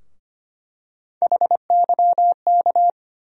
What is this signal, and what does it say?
Morse code, heard as a keyed tone, about 25 wpm, 700 Hz. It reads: HYK
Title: USA Call Sign Postfixes - 3 Letters 25wpm